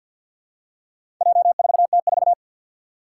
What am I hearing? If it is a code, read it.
Morse code, heard as a keyed tone, about 51 wpm, 700 Hz. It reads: J4T4
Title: Four Challenging Letter-Number Combinations 50wpm